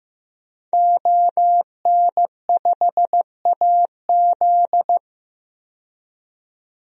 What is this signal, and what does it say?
Morse code, heard as a keyed tone, about 15 wpm, 700 Hz. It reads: ON5AZ
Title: DXCC-100 Call Signs - 5 Characters - 15wpm